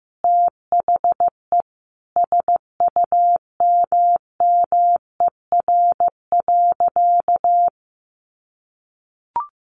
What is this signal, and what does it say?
Morse code, heard as a keyed tone, about 15 wpm, 700 Hz. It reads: THE SUMMER.
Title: Sentences from Top 1500 Words 15wpm